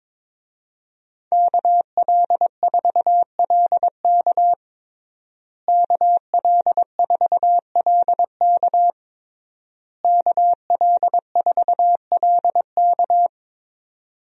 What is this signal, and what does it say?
Morse code, heard as a keyed tone, about 22 wpm, 700 Hz. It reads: KL4LK KL4LK KL4LK
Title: USA Call Signs - 5 Characters - Repeated 3x - 22wpm